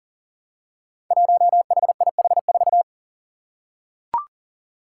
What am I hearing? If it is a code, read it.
Morse code, heard as a keyed tone, about 40 wpm, 700 Hz. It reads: 1HIH4
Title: Five Challenging Letter-Number Combinations 40wpm